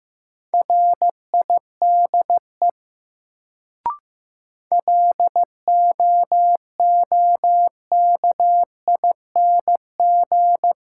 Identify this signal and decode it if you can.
Morse code, heard as a keyed tone, about 15 wpm, 700 Hz. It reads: RIDE LOOKING
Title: Sentences from Top 1000 Words - Review Part 6 - New Word Drill - 15wpm